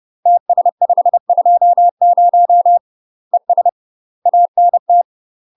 Morse code, heard as a keyed tone, about 30 wpm, 700 Hz. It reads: TS520 ES ANT
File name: UK MASTER QSOs TRAINING / 30WPM / UK MASTER QSOs 0050